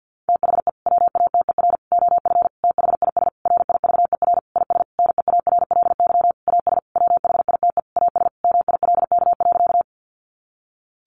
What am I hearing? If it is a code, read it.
Morse code, heard as a keyed tone, about 50 wpm, 700 Hz. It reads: THE WATER OF THIS RIVER IS NEARLY AS WHITE AS MILK.
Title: Sentences from Top 1500 Words 50wpm